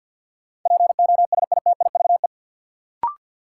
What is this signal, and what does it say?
Morse code, heard as a keyed tone, about 53 wpm, 700 Hz. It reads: POSITIVE
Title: Sentences from Top 1000 Words - Review Part 5 - New Word Drill - 50wpm